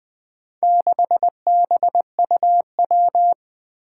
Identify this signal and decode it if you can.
Morse code, heard as a keyed tone, about 20 wpm, 700 Hz. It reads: 6BUW
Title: Four Challenging Letter-Number Combinations 20wpm